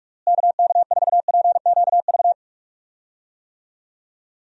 Morse code, heard as a keyed tone, about 46 wpm, 700 Hz. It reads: KK4PQV
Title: USA Call Signs - 6 Characters - Repeated 2x - 45wpm